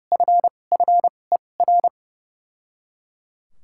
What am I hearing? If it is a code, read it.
Morse code, heard as a keyed tone, about 30 wpm, spaced as for 22 wpm, 700 Hz. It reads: FFER
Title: 2x Character Spacing - Top 200 Words - 30wpm